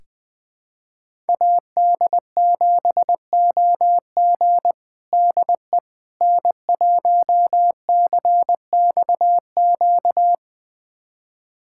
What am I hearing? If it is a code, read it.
Morse code, heard as a keyed tone, about 20 wpm, 700 Hz. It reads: AD7OG DE N1CXQ
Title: USA Callsign DE Callsign - 5 Characters - 20wpm